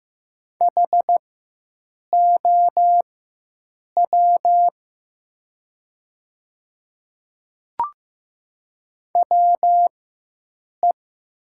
Morse code, heard as a keyed tone, about 15 wpm, spaced as for 7 wpm, 700 Hz. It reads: HOW WE
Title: 4x Character Spacing - Top 300 Words - 15wpm